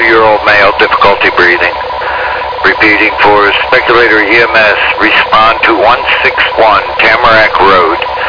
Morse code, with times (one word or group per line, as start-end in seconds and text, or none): none